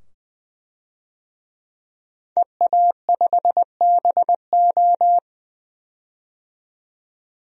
Morse code, heard as a keyed tone, about 21 wpm, 700 Hz. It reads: EA5BO
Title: DXCC-100 Call Signs - 5 Characters - 20wpm